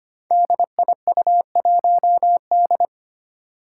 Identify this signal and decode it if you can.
Morse code, heard as a keyed tone, about 25 wpm, 700 Hz. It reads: DIU1D
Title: Five Challenging Letter-Number Combinations 25wpm